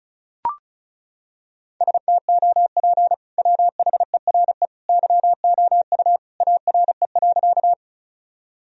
Morse code, heard as a keyed tone, about 34 wpm, 700 Hz. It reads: STOP WHERE YOU ARE.
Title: Sentences from Top 1000 Words - Review Part 1 - 35wpm